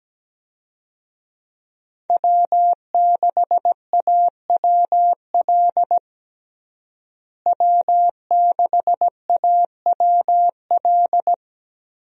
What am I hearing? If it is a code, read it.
Morse code, heard as a keyed tone, about 17 wpm, 700 Hz. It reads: W6AWL W6AWL
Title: USA Call Signs - 5 Characters - Repeated 2x - 17wpm